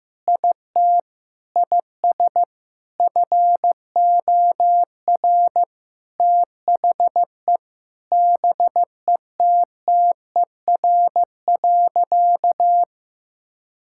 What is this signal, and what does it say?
Morse code, heard as a keyed tone, about 15 wpm, 700 Hz. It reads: IT IS FOR THE BETTER.
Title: Sentences from Top 1000 Words - Review Part 5 - 15wpm